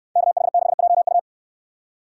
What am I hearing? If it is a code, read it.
Morse code, heard as a keyed tone, about 63 wpm, 700 Hz. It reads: KH6JU